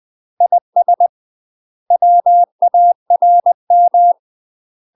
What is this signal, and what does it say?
Morse code, heard as a keyed tone, about 20 wpm, 700 Hz. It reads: IS WARM
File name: UK MASTER QSOs TRAINING / 20WPM / UK MASTER QSOs 0053